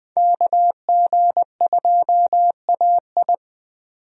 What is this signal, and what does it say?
Morse code, heard as a keyed tone, about 20 wpm, 700 Hz. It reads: KG2AI
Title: USA Call Signs - 5 Characters - 20wpm